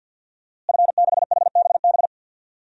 Morse code, heard as a keyed tone, about 48 wpm, 700 Hz. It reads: F6HBB
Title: DXCC-100 Call Signs - 5 Characters - Repeated 2x - 50wpm